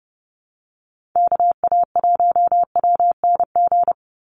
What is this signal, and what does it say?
Morse code, heard as a keyed tone, about 30 wpm, 700 Hz. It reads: KA1WNG
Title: USA Call Signs 30wpm